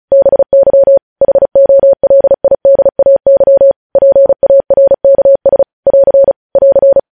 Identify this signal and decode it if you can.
Morse code, heard as a keyed tone, about 36 wpm, 550 Hz. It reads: BY HOLIDAY PARKS <AR> <AR>
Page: news headlines in morse code at 35 wpm